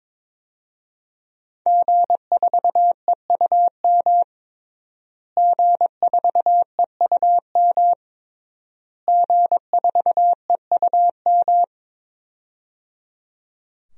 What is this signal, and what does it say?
Morse code, heard as a keyed tone, about 22 wpm, 700 Hz. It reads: G4EUM G4EUM G4EUM
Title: DXCC-100 Call Signs - 5 Characters - Repeated 3x - 22wpm